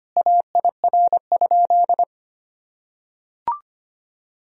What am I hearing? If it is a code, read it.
Morse code, heard as a keyed tone, about 25 wpm, 700 Hz. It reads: AIR?
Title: Sentences from Top 1000 Words - Review Part 1 - 25wpm